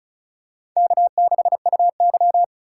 Morse code, heard as a keyed tone, about 36 wpm, 700 Hz. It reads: K6UY
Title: USA Call Signs - 4x Character Spacing - 35wpm